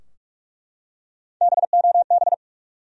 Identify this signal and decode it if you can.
Morse code, heard as a keyed tone, about 48 wpm, 700 Hz. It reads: BOB
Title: USA Common Names - Max 3 Letters - 45wpm